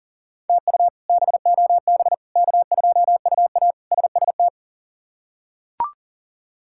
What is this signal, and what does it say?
Morse code, heard as a keyed tone, about 37 wpm, 700 Hz. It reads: TU BOB K2UA SST
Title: K1USN SST QSO Practice - 40wpm